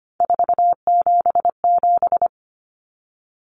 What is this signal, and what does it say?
Morse code, heard as a keyed tone, about 25 wpm, 700 Hz. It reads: 477
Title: Common Cut Numbers - 3-Digit 25wpm